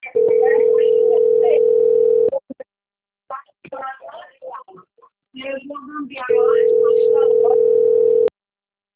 Ringback tone (North America), 440 + 480 Hz.